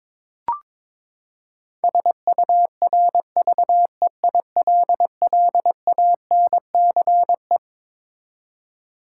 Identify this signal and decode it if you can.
Morse code, heard as a keyed tone, about 22 wpm, 700 Hz. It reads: SURVEILLANCE
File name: Twelve Letter Words - 22wpm